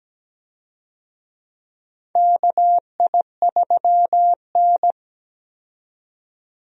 Morse code, heard as a keyed tone, about 17 wpm, 700 Hz. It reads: KI3N
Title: USA Call Signs - 4 Characters - 17wpm